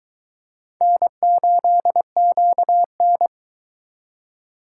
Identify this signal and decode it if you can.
Morse code, heard as a keyed tone, about 23 wpm, 700 Hz. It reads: N8QN